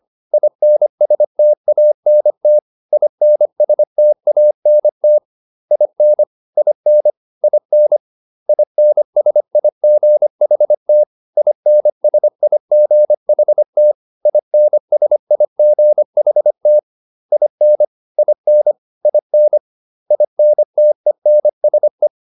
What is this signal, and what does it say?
Morse code, heard as a keyed tone, about 25 wpm, 600 Hz. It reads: INSTANT INSTANT IN IN IN INSIGHT INSIGHT INSIGHT IN IN IN INTENSE